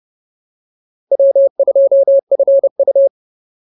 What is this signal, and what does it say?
Morse code, heard as a keyed tone, about 30 wpm, 550 Hz. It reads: W2FU
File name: POTA105 30